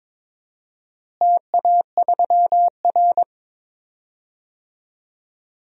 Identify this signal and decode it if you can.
Morse code, heard as a keyed tone, about 21 wpm, 700 Hz. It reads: TA3R
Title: DXCC-100 Call Signs - 4 Characters - Repeated 2x - 22wpm